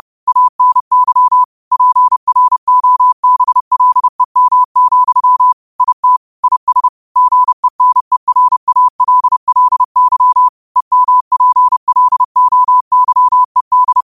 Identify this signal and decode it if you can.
Morse code, heard as a keyed tone, about 30 wpm, 1000 Hz. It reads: ANY PROBLEM, IT IS GENERALLY EMPLOYED
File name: qotdfiles / 30wpm